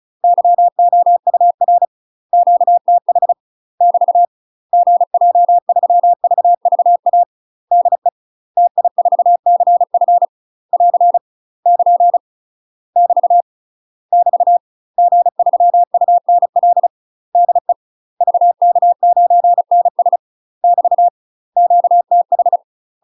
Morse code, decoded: YOUR QTH <BT> GJ3VVA DE TI4CF <AR> <KN> <BT> <BT> G3UNL DE VK9NS <BT> QTH

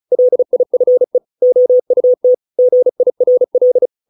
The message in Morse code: LIFE OUT GIRL

